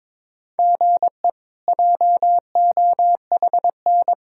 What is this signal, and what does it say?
Morse code, heard as a keyed tone, about 22 wpm, 700 Hz. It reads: GE JOHN
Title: K1USN SST QSO Practice - 22wpm